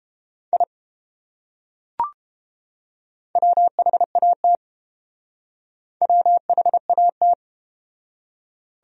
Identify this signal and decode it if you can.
Morse code, heard as a keyed tone, about 30 wpm, 700 Hz. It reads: I WHAT WHAT